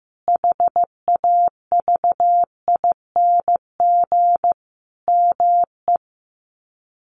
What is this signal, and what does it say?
Morse code, heard as a keyed tone, about 15 wpm, 700 Hz. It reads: HAVING ME
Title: Sets of 5 Words Encore 15wpm